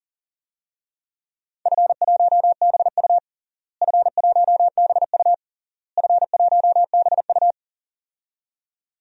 Morse code, heard as a keyed tone, about 40 wpm, 700 Hz. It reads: F1BU F1BU F1BU
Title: DXCC-100 Call Signs - 4 Characters - Repeated 3x - 40wpm